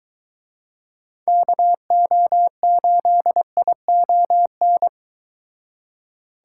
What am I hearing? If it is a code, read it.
Morse code, heard as a keyed tone, about 23 wpm, 700 Hz. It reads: KO8ION